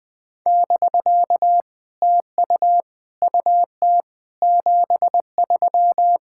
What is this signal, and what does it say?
Morse code, heard as a keyed tone, about 18 wpm, 700 Hz. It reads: <BK> TU UT 73
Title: POTA QSO Practice - 20wpm